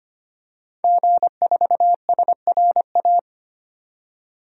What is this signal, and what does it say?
Morse code, heard as a keyed tone, about 25 wpm, 700 Hz. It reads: G4SRA
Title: DXCC-100 Call Signs - 5 Characters - 25wpm